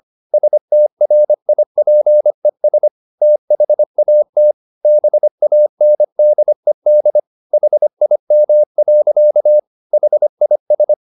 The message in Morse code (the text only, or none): STRIPES THAT BANDED HIM. HIS